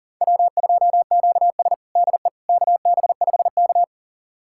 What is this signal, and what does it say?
Morse code, heard as a keyed tone, about 40 wpm, 700 Hz. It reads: W2QS DE KB5K
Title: USA Callsign DE Callsign - 4 Characters - 40wpm